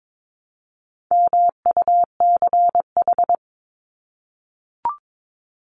Morse code, heard as a keyed tone, about 22 wpm, 700 Hz. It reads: MUCH
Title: Top 300 Words 22wpm